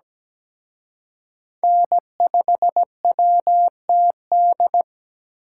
Morse code, heard as a keyed tone, about 20 wpm, 700 Hz. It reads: N5WTD
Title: Speed Racing - USA 1x3 Call Signs 17wpm